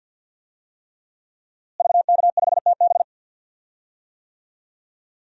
Morse code, heard as a keyed tone, about 46 wpm, 700 Hz. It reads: VK5TB
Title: DXCC-100 Call Signs - 5 Characters - Repeated 3x - 50wpm